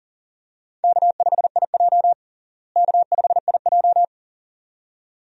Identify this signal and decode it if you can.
Morse code, heard as a keyed tone, about 37 wpm, 700 Hz. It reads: K5IJ K5IJ
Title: USA Call Signs - 4 Characters - Repeated 2x - 40wpm